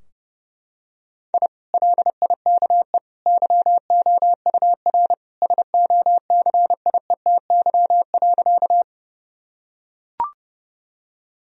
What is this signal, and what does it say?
Morse code, heard as a keyed tone, about 32 wpm, 700 Hz. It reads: I LIKE YOUR SOCIETY.
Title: Sentences from Top 1000 Words - Review Part 4 - 30wpm